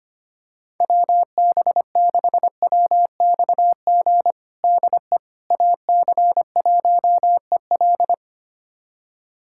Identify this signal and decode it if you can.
Morse code, heard as a keyed tone, about 25 wpm, 700 Hz. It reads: WB6WXG DE AC1EL
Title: USA Callsign DE Callsign 25wpm